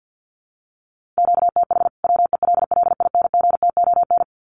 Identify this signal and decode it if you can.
Morse code, heard as a keyed tone, about 50 wpm, 700 Hz. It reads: QTH WELLINGTON
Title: Common QSO Phrases 50wpm